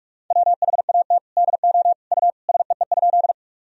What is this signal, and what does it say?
Morse code, heard as a keyed tone, about 46 wpm, 700 Hz. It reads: WHAT DO U SEE?
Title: Sentences from Top 100 Words and Abbreviations 45wpm